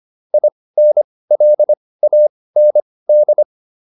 Morse code, heard as a keyed tone, about 25 wpm, spaced as for 18 wpm, 600 Hz. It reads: INLAND